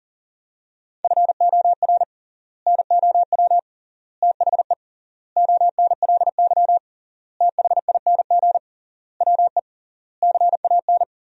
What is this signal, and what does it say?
Morse code, heard as a keyed tone, about 37 wpm, 700 Hz. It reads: FOR NOW THE ONLY THING WE CAN
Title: 3x Word-Spacing - Sentences from Top 200 Words - 40wpm